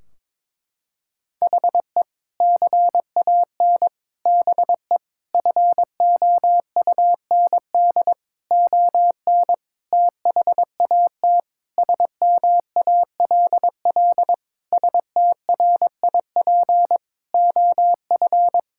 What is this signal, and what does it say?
Morse code, heard as a keyed tone, about 22 wpm, 700 Hz. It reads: HE CAN BE FOUND ON THAT SMALL STRIP OF
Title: Sentences from Top 1600 Words 22wpm